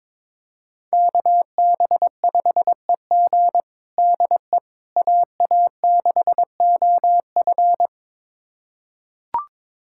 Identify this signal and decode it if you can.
Morse code, heard as a keyed tone, about 22 wpm, 700 Hz. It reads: KB5EG DE AA6OF
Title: USA Callsign DE Callsign - 5 Characters - 22wpm